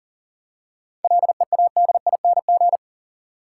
Morse code, heard as a keyed tone, about 40 wpm, 700 Hz. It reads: LEADING
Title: Seven Letter Words - 40wpm